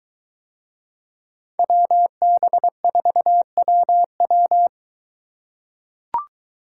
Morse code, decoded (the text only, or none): WB4WW